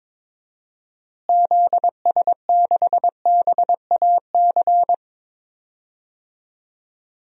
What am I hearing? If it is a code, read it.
Morse code, heard as a keyed tone, about 22 wpm, 700 Hz. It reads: ZS6BAC
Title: DXCC-100 Call Signs - 6 Characters - Repeated 3x - 22wpm